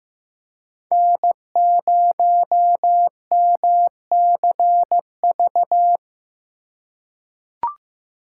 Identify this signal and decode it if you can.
Morse code, heard as a keyed tone, about 15 wpm, 700 Hz. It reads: N0MCV
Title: USA Call Signs - 5 Characters - 4x Character Spacing - 15wpm